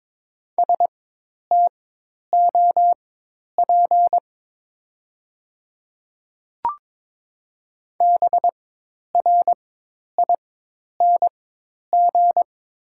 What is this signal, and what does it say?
Morse code, heard as a keyed tone, about 23 wpm, spaced as for 10 wpm, 700 Hz. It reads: STOP BRING
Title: 4x Character Spacing - Top 500 Words - 22wpm